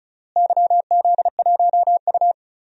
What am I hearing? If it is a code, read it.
Morse code, heard as a keyed tone, about 36 wpm, 700 Hz. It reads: YZ1U
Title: DXCC-100 Call Signs - 4 Characters - Repeated 3x - 35wpm